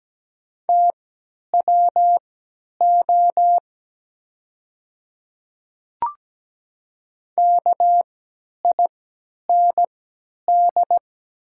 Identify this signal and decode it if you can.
Morse code, heard as a keyed tone, about 17 wpm, spaced as for 10 wpm, 700 Hz. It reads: TWO KIND
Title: 3x Character Spacing - Top 300 Words - 17wpm